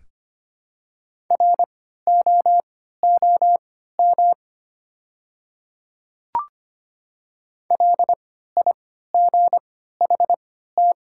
Morse code, heard as a keyed tone, about 25 wpm, spaced as for 14 wpm, 700 Hz. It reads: ROOM LIGHT
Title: 3x Character Spacing - Top 300 Words - 25wpm